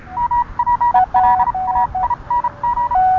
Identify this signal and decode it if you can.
Morse code, heard as a keyed tone, about 31 wpm, spaced as for 27 wpm, 960 Hz. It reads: MPLAING